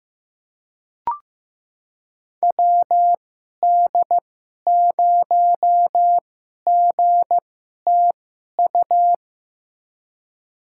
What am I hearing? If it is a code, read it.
Morse code, heard as a keyed tone, about 15 wpm, spaced as for 11 wpm, 700 Hz. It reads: WD0GTU
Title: USA Call Signs - 6 Characters - 2x Character Spacing - 15wpm